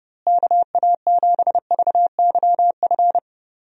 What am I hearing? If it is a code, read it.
Morse code, heard as a keyed tone, about 32 wpm, 700 Hz. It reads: KA7VYF
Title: USA Call Signs - 6 Characters - Repeated 3x - 30wpm